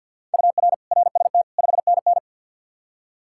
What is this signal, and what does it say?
Morse code, heard as a keyed tone, about 47 wpm, 700 Hz. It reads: UR RST 5NN
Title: Practice QSO - Exchange 1 - Beginning - 50wpm